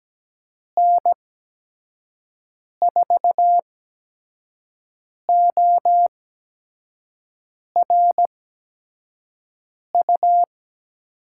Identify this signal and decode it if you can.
Morse code, heard as a keyed tone, about 17 wpm, spaced as for 5 wpm, 700 Hz. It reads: N4ORU
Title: USA Call Signs - 6 Characters - 8x Character Spacing - 17wpm